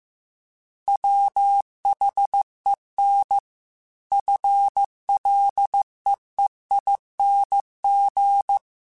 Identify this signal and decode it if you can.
Morse code, heard as a keyed tone, about 15 wpm, 800 Hz. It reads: WHEN FLEEING